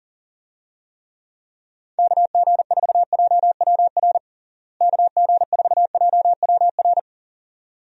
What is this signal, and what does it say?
Morse code, heard as a keyed tone, about 40 wpm, 700 Hz. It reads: KG4JWR KG4JWR
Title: USA Call Signs - 6 Characters - Repeated 2x - 40wpm